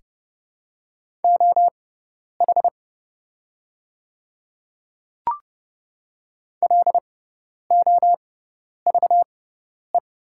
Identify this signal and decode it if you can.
Morse code, heard as a keyed tone, about 30 wpm, spaced as for 10 wpm, 700 Hz. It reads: OH LOVE
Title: 6x Character Spacing - Top 500 Words - 30wpm